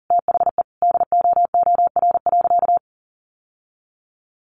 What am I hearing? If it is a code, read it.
Morse code, heard as a keyed tone, about 40 wpm, 700 Hz. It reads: THE DOOR.
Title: Sentences from Top 1200 Words 40wpm